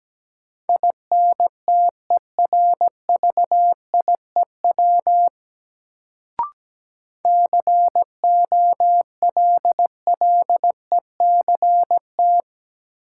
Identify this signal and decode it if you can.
Morse code, heard as a keyed tone, about 17 wpm, 700 Hz. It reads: INTERVIEW COLLECT
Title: Sentences from Top 1600 Words - New Word Drill Alt - 17wpm